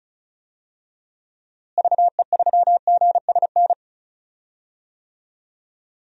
Morse code, heard as a keyed tone, about 34 wpm, 700 Hz. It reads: VE3GSN
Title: DXCC-100 Call Signs - 6 Characters - Repeated 2x - 35wpm